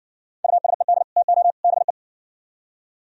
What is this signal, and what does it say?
Morse code, heard as a keyed tone, about 57 wpm, 700 Hz. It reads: USED TO BE